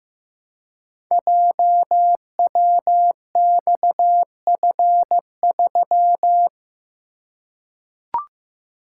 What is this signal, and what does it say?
Morse code, heard as a keyed tone, about 15 wpm, 700 Hz. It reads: JWXF3
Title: Code Groups - 5 Letter:Numbers - 15wpm